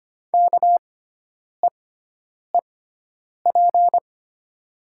Morse code, heard as a keyed tone, about 25 wpm, spaced as for 9 wpm, 700 Hz. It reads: KEEP
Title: 6x Character Spacing - Top 300 Words - 25wpm